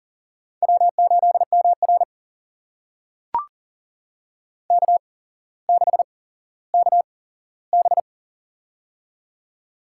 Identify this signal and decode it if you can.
Morse code, heard as a keyed tone, about 39 wpm, spaced as for 17 wpm, 700 Hz. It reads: W8MR K6KB
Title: USA Call Signs - 4 Characters - 8x Character Spacing - 40wpm